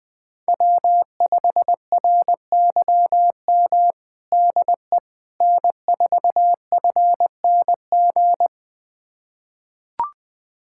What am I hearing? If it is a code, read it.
Morse code, heard as a keyed tone, about 21 wpm, 700 Hz. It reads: W5RYM DE N4FNG
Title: USA Callsign DE Callsign - 5 Characters - 20wpm